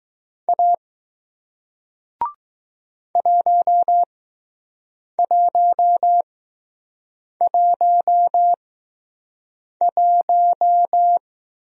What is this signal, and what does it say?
Morse code, heard as a keyed tone, about 20 wpm, spaced as for 6 wpm, 700 Hz. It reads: A 1111